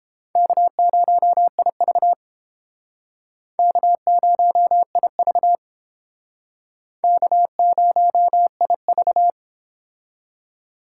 Morse code, heard as a keyed tone, about 32 wpm, 700 Hz. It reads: K0IV K0IV K0IV